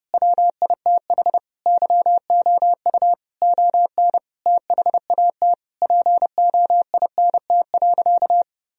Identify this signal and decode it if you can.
Morse code, heard as a keyed tone, about 30 wpm, 700 Hz. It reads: WITH YOU ON THAT POINT.
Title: Sentences from Top 1000 Words - Review Part 1 - 30wpm